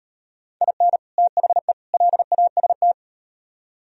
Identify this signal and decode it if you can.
Morse code, heard as a keyed tone, about 38 wpm, 700 Hz. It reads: IN THE LAST